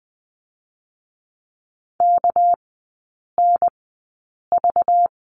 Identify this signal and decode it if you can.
Morse code, heard as a keyed tone, about 20 wpm, spaced as for 8 wpm, 700 Hz. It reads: KNV
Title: 218 Lesson 35e - Number 0 - Three Characters 20wpm